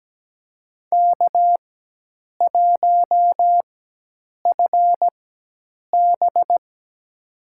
Morse code, heard as a keyed tone, about 17 wpm, spaced as for 8 wpm, 700 Hz. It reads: K1FB
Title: USA Call Signs - 4 Characters - 4x Character Spacing - 17wpm